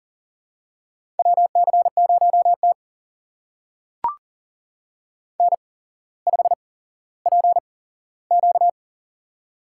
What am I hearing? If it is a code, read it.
Morse code, heard as a keyed tone, about 42 wpm, spaced as for 17 wpm, 700 Hz. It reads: WC0T N5PQ
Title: USA Call Signs - 4 Characters - 8x Character Spacing - 40wpm